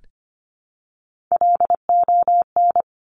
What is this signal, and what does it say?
Morse code, heard as a keyed tone, about 25 wpm, 700 Hz. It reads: LON
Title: Top Three Letter Combinations in Words 25wpm